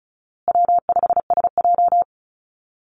Morse code, heard as a keyed tone, about 34 wpm, 700 Hz. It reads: W5SJ
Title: Easy Call Signs Repeated 3x - 35wpm